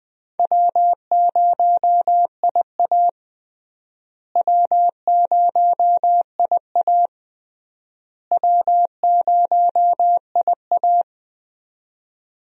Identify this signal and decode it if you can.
Morse code, heard as a keyed tone, about 21 wpm, 700 Hz. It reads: W0IA W0IA W0IA
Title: USA Call Signs - 4 Characters - Repeated 3x - 20wpm